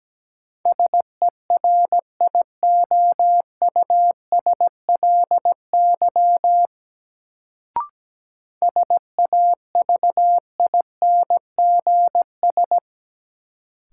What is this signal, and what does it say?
Morse code, heard as a keyed tone, about 17 wpm, 700 Hz. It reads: SERIOUSLY SAVINGS
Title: Sentences from Top 1200 Words - New Word Drill - 17wpm